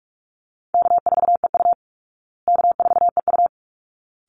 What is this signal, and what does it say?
Morse code, heard as a keyed tone, about 44 wpm, 700 Hz. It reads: K4EU K4EU
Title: Call Signs Repeated 3x - 45wpm